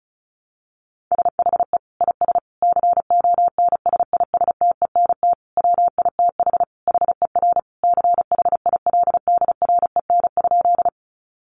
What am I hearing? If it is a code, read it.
Morse code, heard as a keyed tone, about 34 wpm, 700 Hz. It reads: SHE IS CONSISTENT WITH HER CHILDREN?
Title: Sentences from Top 1500 Words 35wpm